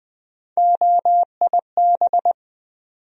Morse code, heard as a keyed tone, about 20 wpm, 700 Hz. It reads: OIB
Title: USA Call Sign Postfixes - 3 Letters 20wpm